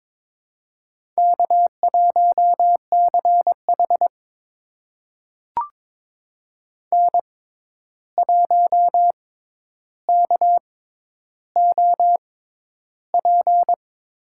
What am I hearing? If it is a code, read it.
Morse code, heard as a keyed tone, about 22 wpm, spaced as for 11 wpm, 700 Hz. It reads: K1CH N1KOP
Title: USA Call Signs - 6x Character Spacing - 22wpm